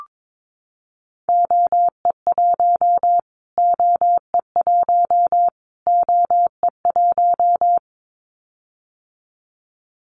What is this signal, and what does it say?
Morse code, heard as a keyed tone, about 22 wpm, 700 Hz. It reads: OE1 OE1 OE1
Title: Callsign Prefixes for Top 25 DXCC - Repeated 3x - 22wpm